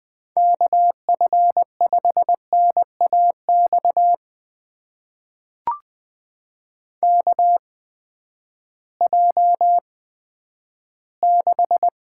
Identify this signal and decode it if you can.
Morse code, heard as a keyed tone, about 20 wpm, spaced as for 11 wpm, 700 Hz. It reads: KF5NAX KJ6
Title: USA Call Signs - 6 Characters - 8x Character Spacing - 20wpm